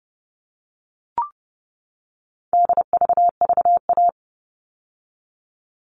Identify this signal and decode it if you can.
Morse code, heard as a keyed tone, about 30 wpm, 700 Hz. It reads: DVVA
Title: All Cut Numbers - 4-Digit 30wpm